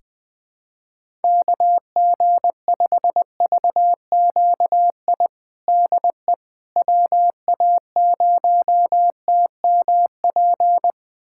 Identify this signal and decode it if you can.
Morse code, heard as a keyed tone, about 20 wpm, 700 Hz. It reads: KG5VQI DE WA0TMP
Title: USA Callsign DE Callsign 20wpm